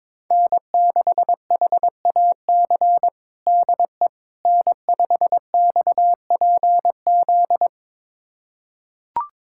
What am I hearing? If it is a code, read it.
Morse code, heard as a keyed tone, about 22 wpm, 700 Hz. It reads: N6HAC DE N5XPZ
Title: USA Callsign DE Callsign - 5 Characters - 22wpm